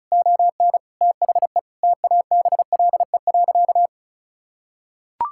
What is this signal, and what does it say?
Morse code, heard as a keyed tone, about 36 wpm, 700 Hz. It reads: ON THE TABLE.
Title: Sentences from Top 1000 Words - Review Part 5 - 35wpm